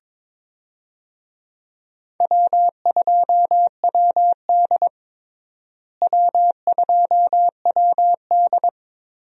Morse code, heard as a keyed tone, about 21 wpm, 700 Hz. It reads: W2WD W2WD
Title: USA Call Signs - 4 Characters - Repeated 2x - 22wpm